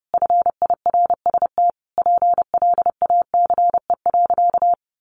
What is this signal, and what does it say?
Morse code, heard as a keyed tone, about 30 wpm, 700 Hz. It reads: FIRST PLACE.
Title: Additional Easy Phrases 30wpm